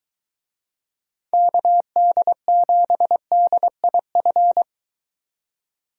Morse code, heard as a keyed tone, about 22 wpm, 700 Hz. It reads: KD7DIF